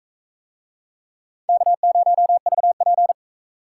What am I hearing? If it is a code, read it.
Morse code, heard as a keyed tone, about 44 wpm, 700 Hz. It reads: K0VP